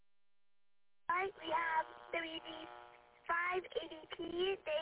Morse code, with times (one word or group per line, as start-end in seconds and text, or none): none